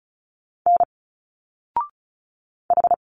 Morse code, heard as a keyed tone, about 34 wpm, 700 Hz. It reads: N H
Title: Single Letters 35wpm